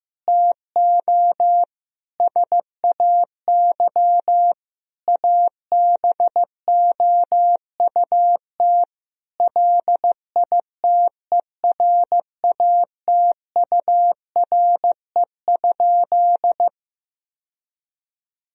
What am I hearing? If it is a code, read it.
Morse code, heard as a keyed tone, about 15 wpm, 700 Hz. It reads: TO SAY ABOUT LITERATURE?
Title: Sentences from Top 1000 Words - Review Part 3 - 15wpm